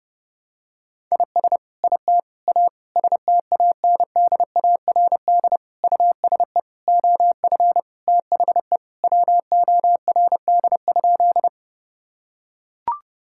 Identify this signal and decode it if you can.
Morse code, heard as a keyed tone, about 30 wpm, 700 Hz. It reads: IS IT A STANDARD USE OF THE WORD?
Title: Sentences from Top 1000 Words - Review Part 6 - 30wpm